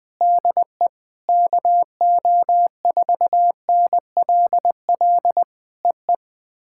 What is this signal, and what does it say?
Morse code, heard as a keyed tone, about 20 wpm, 700 Hz. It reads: DE KO4NLL EE
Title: POTA QSO Practice - 20wpm